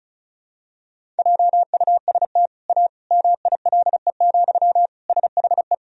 Morse code, heard as a keyed tone, about 34 wpm, 700 Hz. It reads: JUST A MILE, SHE
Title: Sentences from Top 1000 Words - Review Part 1 - 35wpm